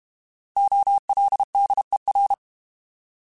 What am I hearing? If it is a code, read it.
Morse code, heard as a keyed tone, about 32 wpm, 800 Hz. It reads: OLDER
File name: Alphabetical / Alphabetical mp3s / Os @30